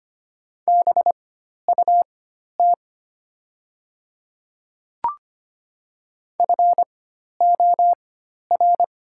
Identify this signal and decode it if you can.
Morse code, heard as a keyed tone, about 25 wpm, spaced as for 12 wpm, 700 Hz. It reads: BUT FOR
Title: 4x Character Spacing - Top 300 Words - 25wpm